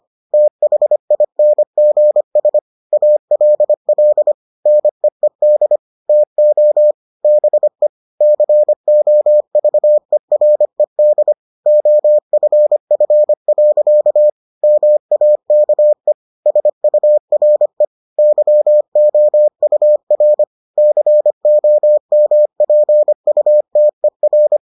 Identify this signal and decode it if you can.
Morse code, heard as a keyed tone, about 25 wpm, 600 Hz. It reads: THINGS ALL NEED TO BE COVERED OFF. MAKE SURE YOUR COMPUTER